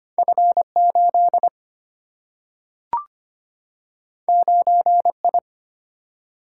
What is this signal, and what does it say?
Morse code, heard as a keyed tone, about 25 wpm, 700 Hz. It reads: F8 9I
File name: Two Letter-Number Combinations - 25wpm